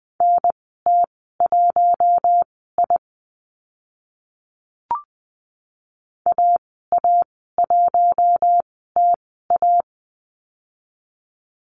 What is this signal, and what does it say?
Morse code, heard as a keyed tone, about 20 wpm, spaced as for 14 wpm, 700 Hz. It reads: NT1I AA1TA
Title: 042 Lesson 8g - Number 1 - Call Signs 20wpm